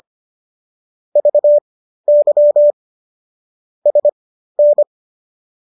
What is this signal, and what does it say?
Morse code, heard as a keyed tone, about 25 wpm, spaced as for 13 wpm, 590 Hz. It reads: VY SN